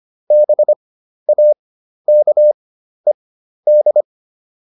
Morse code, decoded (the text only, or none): BAKED